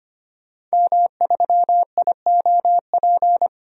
Morse code, heard as a keyed tone, about 25 wpm, 690 Hz. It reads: M3IOP
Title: DXCC-100 Call Signs - 5 Characters - 25wpm